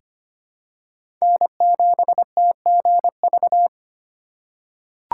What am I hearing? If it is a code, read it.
Morse code, heard as a keyed tone, about 25 wpm, 700 Hz. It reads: N7TGV
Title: USA Call Signs - 5 Characters - 2x Character Spacing - 25wpm